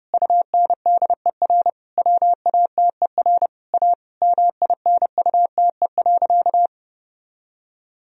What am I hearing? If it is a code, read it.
Morse code, heard as a keyed tone, about 32 wpm, 700 Hz. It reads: UNDER WATER A MINUTE.
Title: Sentences from Top 1000 Words - Review Part 1 - 30wpm